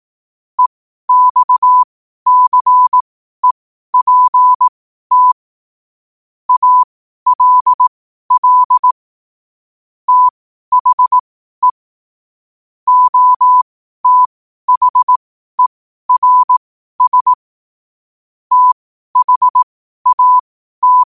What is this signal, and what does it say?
Morse code, heard as a keyed tone, about 18 wpm, spaced as for 12 wpm, 1000 Hz. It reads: EXCEPT ALL THE OTHERS THAT
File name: qotdfiles / 12wpm